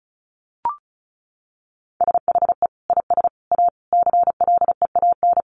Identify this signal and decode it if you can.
Morse code, heard as a keyed tone, about 34 wpm, 700 Hz. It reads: SHE IS A CLEAN
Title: Sentences from Top 1100 Words 35wpm